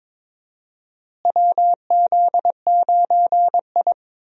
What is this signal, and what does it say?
Morse code, heard as a keyed tone, about 22 wpm, 700 Hz. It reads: WZ9I
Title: USA Call Signs - 4 Characters - Repeated 3x - 22wpm